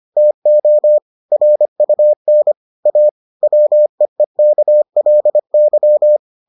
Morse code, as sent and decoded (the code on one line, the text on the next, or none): - --- / .-. ..- -. / .- / .-- . -.- .-.. -.--
TO RUN A WEEKLY